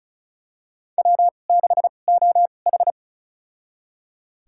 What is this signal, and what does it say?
Morse code, heard as a keyed tone, about 36 wpm, spaced as for 25 wpm, 700 Hz. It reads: W6OH
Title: USA Call Signs - 4 Characters - 2x Character Spacing - 35wpm